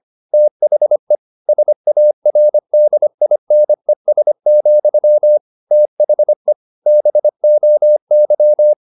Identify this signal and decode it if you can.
Morse code, heard as a keyed tone, about 25 wpm, 600 Hz. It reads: THE SARDINES, THE BOY
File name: the old man and the sea part 1 25wpm